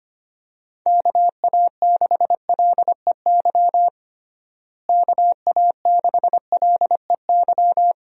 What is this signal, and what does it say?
Morse code, heard as a keyed tone, about 25 wpm, 700 Hz. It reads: KA6LEY KA6LEY
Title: USA Call Signs - 6 Characters - Repeated 2x - 25wpm